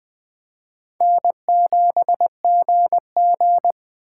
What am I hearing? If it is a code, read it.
Morse code, heard as a keyed tone, about 21 wpm, 700 Hz. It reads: N7GG
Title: Speed Racing - USA 1x2 Call Signs 17wpm